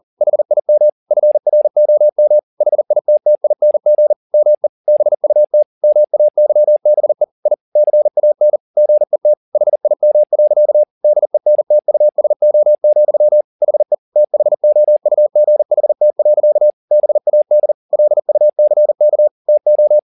Morse code, heard as a keyed tone, about 40 wpm, 600 Hz. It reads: HIM FROM HITTING ME BUT MAYBE I CAN GET HIM. DENTUSO, HE THOUGHT. BAD LUCK TO
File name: the old man and the sea 2 40wpm